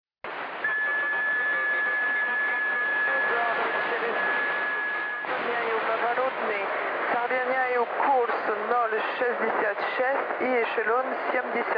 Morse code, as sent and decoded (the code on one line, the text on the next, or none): none